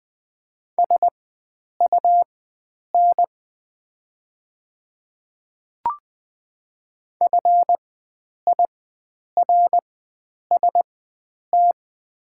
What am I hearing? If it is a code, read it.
Morse code, heard as a keyed tone, about 20 wpm, spaced as for 9 wpm, 700 Hz. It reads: SUN FIRST
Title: 4x Character Spacing - Top 300 Words - 20wpm